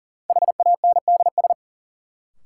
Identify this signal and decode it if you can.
Morse code, heard as a keyed tone, about 40 wpm, 700 Hz. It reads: HANDS
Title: Sentences from Top 1000 Words - Review Part 6 - New Word Drill - 40wpm